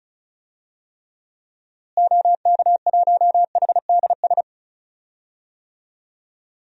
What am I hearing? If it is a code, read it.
Morse code, heard as a keyed tone, about 36 wpm, 700 Hz. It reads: OK1HDS
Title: DXCC-100 Call Signs - 6 Characters - Repeated 3x - 35wpm